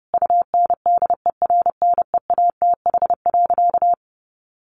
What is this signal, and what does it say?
Morse code, heard as a keyed tone, about 30 wpm, 700 Hz. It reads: UNDERNEATH.
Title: A Sound Of Thunder 30wpm